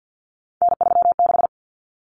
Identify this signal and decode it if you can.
Morse code, heard as a keyed tone, about 50 wpm, 700 Hz. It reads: N36
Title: Common Cut Numbers - 3-Digit 50wpm